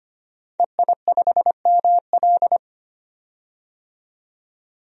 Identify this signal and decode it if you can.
Morse code, heard as a keyed tone, about 25 wpm, 700 Hz. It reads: EI5ML
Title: DXCC-100 Call Signs - 5 Characters - 25wpm